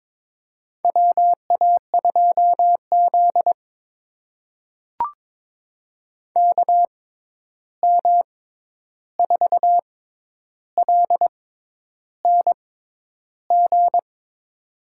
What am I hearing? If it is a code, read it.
Morse code, heard as a keyed tone, about 21 wpm, spaced as for 10 wpm, 700 Hz. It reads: WA2Z KM4LNG
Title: USA Call Signs - 6x Character Spacing - 22wpm